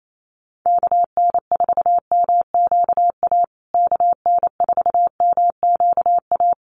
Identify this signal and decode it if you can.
Morse code, heard as a keyed tone, about 28 wpm, 700 Hz. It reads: KN4MQA KN4MQA
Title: USA Call Signs Repeated 3x 28wpm